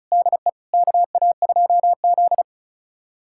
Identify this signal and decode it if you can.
Morse code, heard as a keyed tone, about 34 wpm, 700 Hz. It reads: DE KA2Z
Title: USA Callsign DE Callsign - 4 Characters - 35wpm